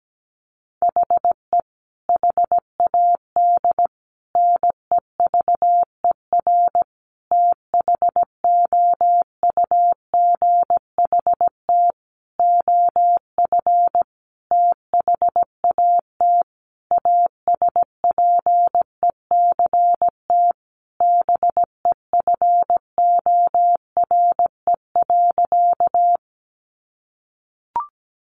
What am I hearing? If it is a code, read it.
Morse code, heard as a keyed tone, about 17 wpm, 700 Hz. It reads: HE HAD NEVER THOUGHT OF THAT ASPECT BEFORE.
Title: Sentences from Top 1300 Words 17wpm